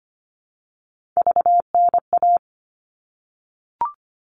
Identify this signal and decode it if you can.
Morse code, heard as a keyed tone, about 25 wpm, 700 Hz. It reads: VNA
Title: All Cut Numbers - 3-Digit 25wpm